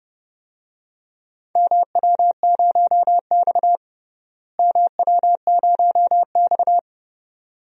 Morse code, heard as a keyed tone, about 30 wpm, 700 Hz. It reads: MW0X MW0X
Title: DXCC-100 Call Signs - 4 Characters - Repeated 2x - 30wpm